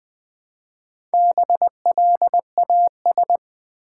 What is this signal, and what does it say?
Morse code, heard as a keyed tone, about 20 wpm, 700 Hz. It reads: BLAS
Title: Build A Word - 1 Letter At A Time - 3 to 5-Letter Words - 20wpm